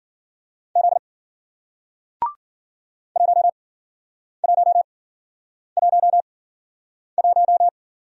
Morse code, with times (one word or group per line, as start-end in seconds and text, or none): 0.8-1.0 Z
3.2-7.7 1111